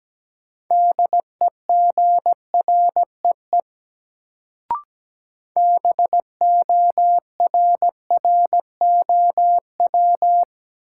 Morse code, heard as a keyed tone, about 17 wpm, 700 Hz. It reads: DEGREE BORROW
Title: Sentences from Top 1500 Words - New Word Drill Alt - 17wpm